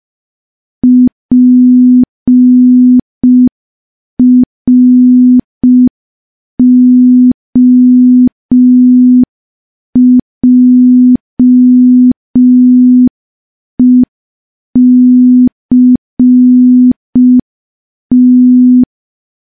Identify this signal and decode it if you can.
Morse code, heard as a keyed tone, about 5 wpm, 250 Hz. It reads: PROJECT
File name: morse / morse-training-05WPM